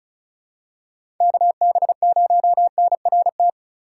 Morse code, heard as a keyed tone, about 36 wpm, 700 Hz. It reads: KB0NRT
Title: USA Call Signs - 6 Characters - 3x Character Spacing - 35wpm